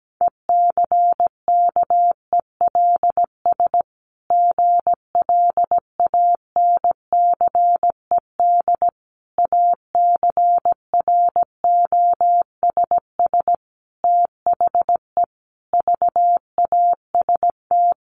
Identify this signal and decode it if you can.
Morse code, heard as a keyed tone, about 17 wpm, 700 Hz. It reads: ECKELS GLANCED ACROSS THE VAST
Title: A Sound Of Thunder 17wpm